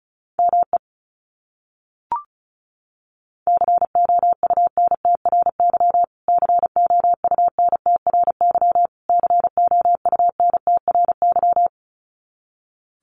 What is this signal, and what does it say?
Morse code, heard as a keyed tone, about 34 wpm, 700 Hz. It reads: ME COUNTRY COUNTRY COUNTRY
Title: Top 300 Words Repeated 35wpm